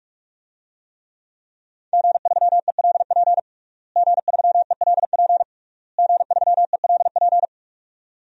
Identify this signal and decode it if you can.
Morse code, heard as a keyed tone, about 46 wpm, 700 Hz. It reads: G3ELP G3ELP G3ELP
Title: DXCC-100 Call Signs - 5 Characters - Repeated 3x - 45wpm